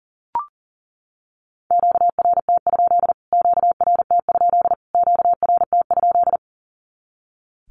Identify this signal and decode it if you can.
Morse code, heard as a keyed tone, about 40 wpm, 700 Hz. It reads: QRT? QRT? QRT?
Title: Q-Codes Repeated 3x 40wpm